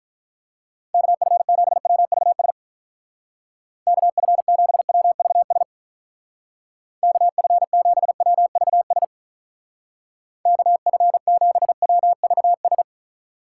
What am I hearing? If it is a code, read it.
Morse code, heard as a keyed tone, about 46 wpm, 700 Hz. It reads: KF7WVS KF7WVS KF7WVS KF7WVS